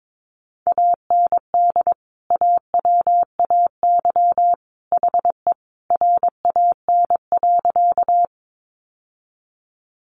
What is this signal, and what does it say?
Morse code, heard as a keyed tone, about 22 wpm, 700 Hz. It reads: AND AWAY HE RAN.
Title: Sentences from Top 600 Words 22wpm